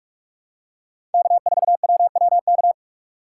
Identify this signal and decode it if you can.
Morse code, heard as a keyed tone, about 44 wpm, 700 Hz. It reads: K4WWK
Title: USA Call Signs - 5 Characters - 6x Character Spacing - 45wpm